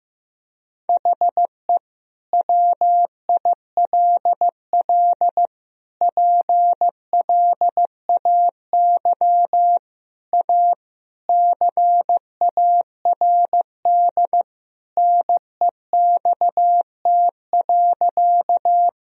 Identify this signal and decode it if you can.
Morse code, heard as a keyed tone, about 15 wpm, 700 Hz. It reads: HE WILL PLAY A CARD NEXT.
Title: Sentences from Top 1000 Words - Review Part 4 - 15wpm